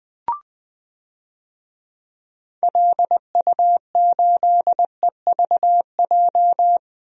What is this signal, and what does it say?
Morse code, heard as a keyed tone, about 20 wpm, 700 Hz. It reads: LU8EVJ
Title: DXCC-100 Call Signs - 6 Characters - Repeated 2x - 20wpm